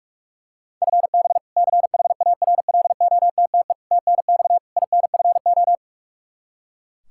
Morse code, heard as a keyed tone, about 41 wpm, 700 Hz. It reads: FB CHARLOTTE TNX INFO
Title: Practice QSO - Exchange 3 - More Info - 45wpm